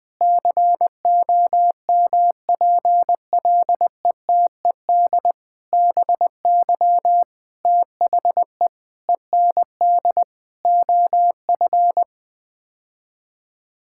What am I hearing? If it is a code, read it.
Morse code, heard as a keyed tone, about 20 wpm, 700 Hz. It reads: COMPLETED BY THE END OF
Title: Sets of 8 Words - 20wpm